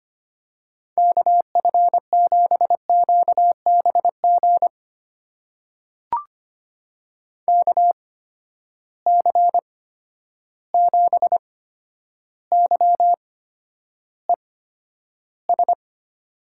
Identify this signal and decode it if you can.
Morse code, heard as a keyed tone, about 25 wpm, spaced as for 11 wpm, 700 Hz. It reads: KF7QBG KC7YES
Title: USA Call Signs - 6 Characters - 8x Character Spacing - 25wpm